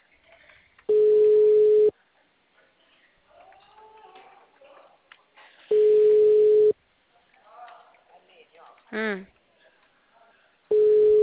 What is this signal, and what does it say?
Ringback tone (Europe, 425 Hz), with speech.
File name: out- -2001- - - .37907